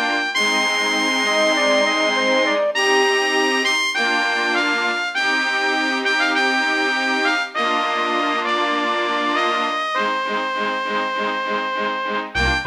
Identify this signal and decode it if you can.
Dial-up modem handshake.